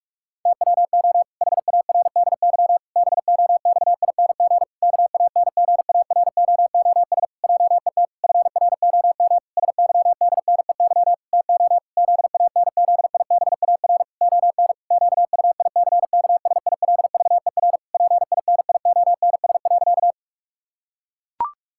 Morse code, heard as a keyed tone, about 41 wpm, 700 Hz. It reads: TWO HARDY BOXING KANGAROOS JET FROM SYDNEY TO ZANZIBAR ON QUICKSILVER PINIONS.
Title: Pangram Challenge v2 - 45wpm